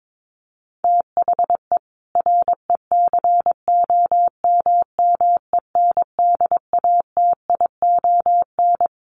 Morse code, heard as a keyed tone, about 22 wpm, 700 Hz. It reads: THE RECOMMENDATION
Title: Sentences from Top 1400 Words 22wpm